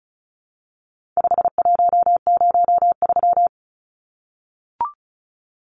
Morse code, heard as a keyed tone, about 34 wpm, 700 Hz. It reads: 5103
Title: Numbers - 4-Digit 35wpm